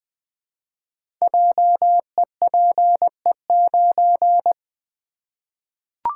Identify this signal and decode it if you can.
Morse code, heard as a keyed tone, about 20 wpm, 700 Hz. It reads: JEPE9
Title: Code Groups - 5 Letter:Numbers - 20wpm